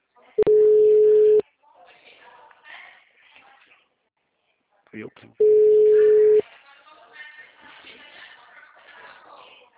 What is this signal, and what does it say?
Ringback tone (Europe, 425 Hz), with speech.